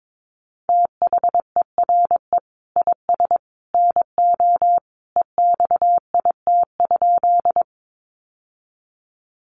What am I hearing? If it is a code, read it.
Morse code, heard as a keyed tone, about 22 wpm, 700 Hz. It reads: THERE IS NO EXIT?
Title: Sentences from Top 1100 Words 22wpm